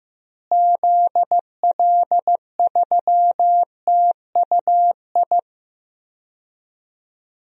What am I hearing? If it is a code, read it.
Morse code, heard as a keyed tone, about 15 wpm, 700 Hz. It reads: ZL3TUI
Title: DXCC-100 Call Signs - 6 Characters - Repeated 2x - 15wpm